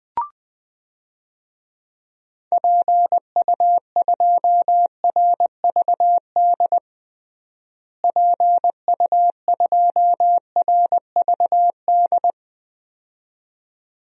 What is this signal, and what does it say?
Morse code, heard as a keyed tone, about 20 wpm, 700 Hz. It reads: PU2RVD PU2RVD
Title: DXCC-100 Call Signs - 6 Characters - Repeated 2x - 20wpm